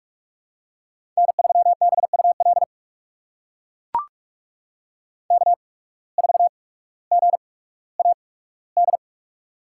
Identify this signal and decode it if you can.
Morse code, heard as a keyed tone, about 42 wpm, spaced as for 19 wpm, 700 Hz. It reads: N3BUL K4GAD
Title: USA Call Signs - 5 Characters - 8x Character Spacing - 45wpm